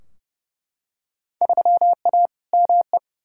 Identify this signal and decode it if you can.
Morse code, heard as a keyed tone, about 32 wpm, 700 Hz. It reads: 3A ME
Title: Field Day Exchanges - 30wpm